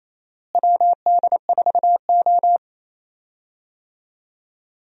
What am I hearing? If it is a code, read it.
Morse code, heard as a keyed tone, about 28 wpm, 700 Hz. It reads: WD4O
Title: USA Call Signs - 4 Characters - Repeated 3x - 28wpm